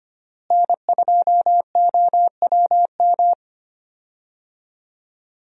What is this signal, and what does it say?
Morse code, heard as a keyed tone, about 25 wpm, 700 Hz. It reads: N2OWM
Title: USA Call Signs - 5 Characters - Repeated 3x - 25wpm